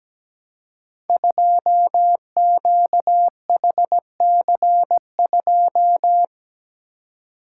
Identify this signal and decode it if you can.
Morse code, heard as a keyed tone, about 17 wpm, 700 Hz. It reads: 2QHC2
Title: Code Groups - 5 Letter:Numbers - 17wpm